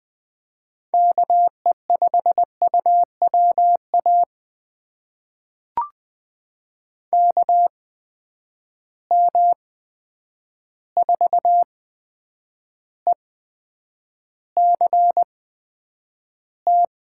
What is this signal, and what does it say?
Morse code, heard as a keyed tone, about 20 wpm, spaced as for 9 wpm, 700 Hz. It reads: KE5UWA KM4ECT
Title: USA Call Signs - 6 Characters - 8x Character Spacing - 20wpm